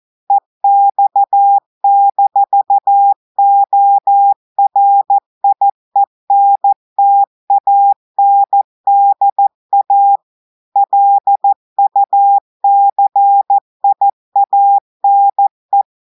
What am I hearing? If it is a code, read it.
Morse code, heard as a keyed tone, about 14 wpm, 800 Hz. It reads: EX-ORIENTANDA LUCIANE
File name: audios / portugais / portugais texte en clair 1 vitesse 14 mots minutes